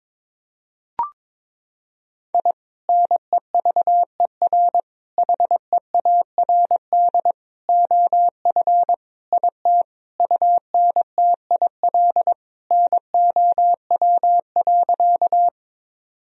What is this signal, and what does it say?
Morse code, heard as a keyed tone, about 22 wpm, 700 Hz. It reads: I NEVER HEARD OF IT UNTIL NOW.
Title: Sentences from Top 1000 Words - Review Part 1 - 22wpm